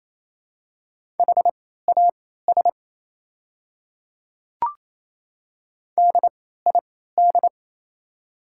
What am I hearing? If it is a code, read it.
Morse code, heard as a keyed tone, about 28 wpm, spaced as for 16 wpm, 700 Hz. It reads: HAS DID
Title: 3x Character Spacing - Top 100 Words - 28wpm